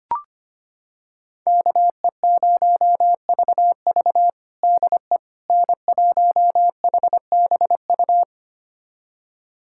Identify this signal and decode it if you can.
Morse code, heard as a keyed tone, about 25 wpm, 700 Hz. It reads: KE0VV DE N1HBU
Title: USA Callsign DE Callsign - 5 Characters - 25wpm